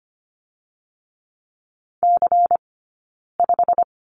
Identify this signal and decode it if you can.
Morse code, heard as a keyed tone, about 25 wpm, 700 Hz. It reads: C 5
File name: course / 25:10wpm / 092 Lesson 16d - Letter c - Two Characters 25wpm